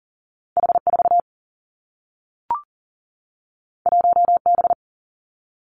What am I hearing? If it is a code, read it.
Morse code, heard as a keyed tone, about 38 wpm, 700 Hz. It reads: H 4 1 B
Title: Two Challenging Letter-Number Combinations 40wpm